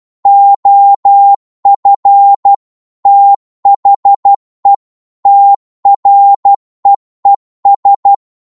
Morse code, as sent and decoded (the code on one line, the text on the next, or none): --- ..-. / - .... / - .-. . ...
OF THE TREES